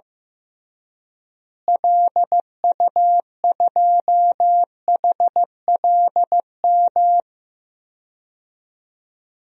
Morse code, heard as a keyed tone, about 16 wpm, 700 Hz. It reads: LU2HLM
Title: DXCC-100 Call Signs - 6 Characters - Repeated 3x - 15wpm